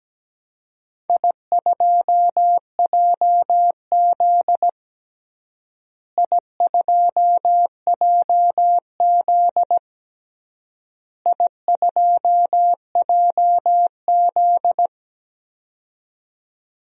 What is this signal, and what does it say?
Morse code, heard as a keyed tone, about 17 wpm, 700 Hz. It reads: I2JZ I2JZ I2JZ
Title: DXCC-100 Call Signs - 4 Characters - Repeated 3x - 17wpm